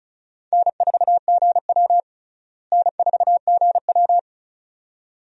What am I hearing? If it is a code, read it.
Morse code, heard as a keyed tone, about 34 wpm, 700 Hz. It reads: N4GW N4GW
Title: USA Call Signs - 4 Characters - Repeated 2x - 35wpm